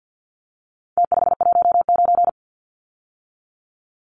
Morse code, heard as a keyed tone, about 48 wpm, 700 Hz. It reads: T519
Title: Numbers - Contest Style - 4-Digit 50wpm